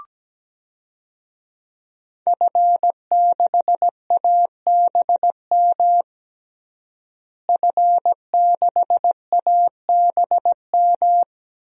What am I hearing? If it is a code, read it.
Morse code, heard as a keyed tone, about 17 wpm, 700 Hz. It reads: F6ABM F6ABM
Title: DXCC-100 Call Signs - 5 Characters - Repeated 3x - 17wpm